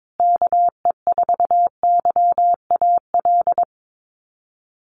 Morse code, heard as a keyed tone, about 22 wpm, 700 Hz. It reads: KE4YAL
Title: USA Call Signs Repeated 3x 22wpm